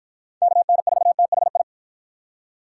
Morse code, heard as a keyed tone, about 52 wpm, 700 Hz. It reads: KN4THI